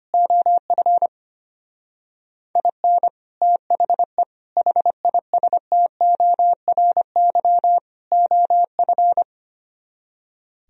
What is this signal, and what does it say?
Morse code, heard as a keyed tone, about 31 wpm, 700 Hz. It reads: OF IN THE HISTORY OF
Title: Speed-Racing - Sets of 4 Words 25wpm